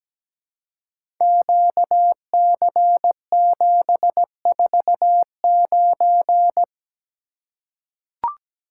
Morse code, heard as a keyed tone, about 17 wpm, 700 Hz. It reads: QC749
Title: Code Groups - 5 Letter:Numbers - 17wpm